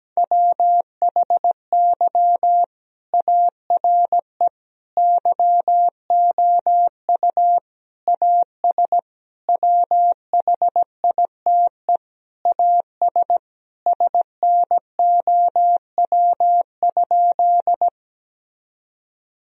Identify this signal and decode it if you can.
Morse code, heard as a keyed tone, about 17 wpm, 700 Hz. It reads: WHY ARE YOU AS WHITE AS SNOW?
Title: Sentences from Top 1000 Words - Review Part 3 - 17wpm